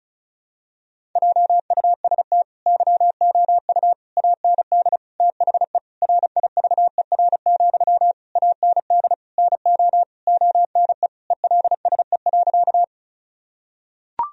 Morse code, heard as a keyed tone, about 34 wpm, 700 Hz. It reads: JUST YOU AND THE RIVER, AND NO ONE ELSE.
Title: Sentences from Top 1000 Words - Review Part 1 - 35wpm